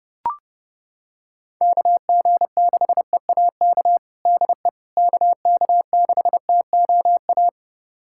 Morse code, heard as a keyed tone, about 32 wpm, 700 Hz. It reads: KG6EAK DE KK6TOA
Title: USA Callsign DE Callsign - 6 Characters - 30wpm